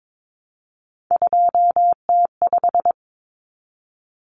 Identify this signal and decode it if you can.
Morse code, heard as a keyed tone, about 22 wpm, 700 Hz. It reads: 2T5
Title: Common Cut Numbers - 3-Digit 22wpm